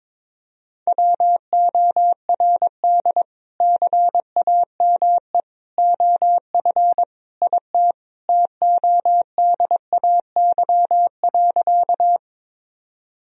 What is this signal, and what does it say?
Morse code, heard as a keyed tone, about 22 wpm, 700 Hz. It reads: WORD CAME OF IT TODAY.
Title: Sentences from Top 1000 Words - Review Part 4 - 22wpm